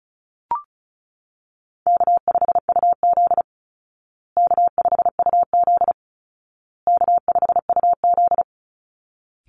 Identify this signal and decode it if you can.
Morse code, heard as a keyed tone, about 34 wpm, 700 Hz. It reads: K5UZ K5UZ K5UZ
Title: Easy Call Signs Repeated 3x - 35wpm